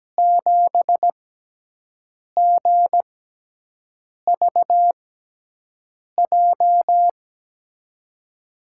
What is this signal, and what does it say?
Morse code, heard as a keyed tone, about 17 wpm, spaced as for 6 wpm, 700 Hz. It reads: 7GVJ
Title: USA Call Signs - 6 Characters - 6x Character Spacing - 17wpm